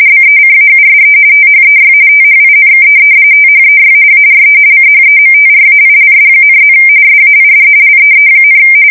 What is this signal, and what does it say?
RTTY, about 45.45 baud. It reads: BECAUSE THEY THINK THEY ARE CHEMTRAILS,+ SAID LEE,
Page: 170 Shift 2125/2295